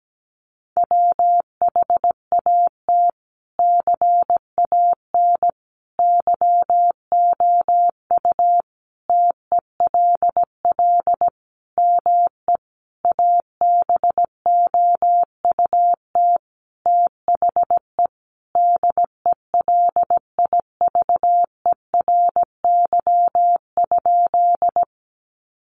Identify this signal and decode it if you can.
Morse code, heard as a keyed tone, about 17 wpm, 700 Hz. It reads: WHAT CAN YOU TELL ME ABOUT THE DELIVERY?
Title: Sentences from Top 1500 Words 17wpm